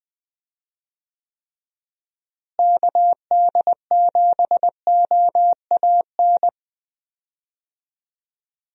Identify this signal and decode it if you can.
Morse code, heard as a keyed tone, about 20 wpm, 700 Hz. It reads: KD7OAN
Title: USA Call Signs - 6 Characters - Repeated 3x - 20wpm